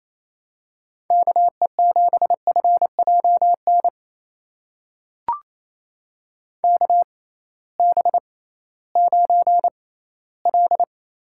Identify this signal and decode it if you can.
Morse code, heard as a keyed tone, about 28 wpm, spaced as for 16 wpm, 700 Hz. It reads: KE7FJN KB9L
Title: USA Call Signs - 6x Character Spacing - 28wpm